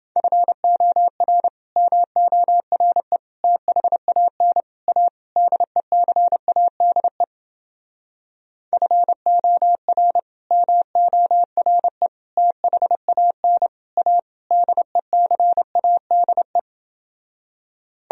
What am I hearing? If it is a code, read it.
Morse code, heard as a keyed tone, about 29 wpm, 700 Hz. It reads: FOR MORE THAN A DECADE FOR MORE THAN A DECADE